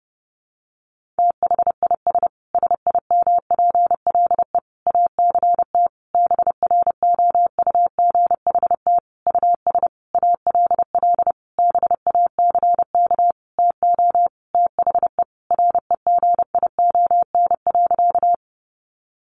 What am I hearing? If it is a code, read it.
Morse code, heard as a keyed tone, about 30 wpm, 700 Hz. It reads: THIS SIMPLE ACT BROUGHT US ALL BACK TO THE REGION.
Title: Sentences from Top 1500 Words 30wpm